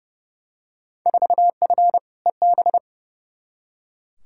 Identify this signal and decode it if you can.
Morse code, heard as a keyed tone, about 30 wpm, 700 Hz. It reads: 4F EB
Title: Field Day Exchanges - 30wpm